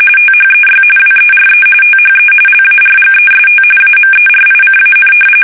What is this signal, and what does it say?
RTTY, about 45.45 baud. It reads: THEY WERE THE LARGEST AMERICAN
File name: fsk850 / battleshipnj